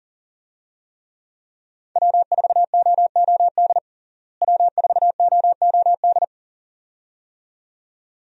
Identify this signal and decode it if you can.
Morse code, heard as a keyed tone, about 40 wpm, 700 Hz. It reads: W4OOD W4OOD
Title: USA Call Signs - 5 Characters - Repeated 2x - 40wpm